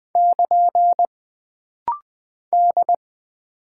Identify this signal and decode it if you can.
Morse code, heard as a keyed tone, about 20 wpm, 700 Hz. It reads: <KN> D
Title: Single Character (Letter:Number:Punctuation:Prosign) - Rapid-Fire - 20wpm